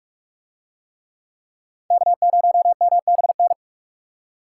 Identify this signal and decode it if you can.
Morse code, heard as a keyed tone, about 45 wpm, 700 Hz. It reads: K0MBN